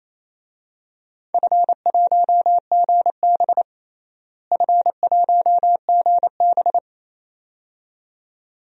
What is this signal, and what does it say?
Morse code, heard as a keyed tone, about 28 wpm, 700 Hz. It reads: F1GB F1GB
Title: DXCC-100 Call Signs - 4 Characters - Repeated 2x - 28wpm